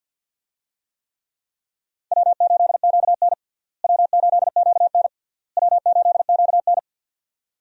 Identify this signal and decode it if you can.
Morse code, heard as a keyed tone, about 51 wpm, 700 Hz. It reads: W8QN W8QN W8QN
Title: USA Call Signs - 4 Characters - Repeated 3x - 50wpm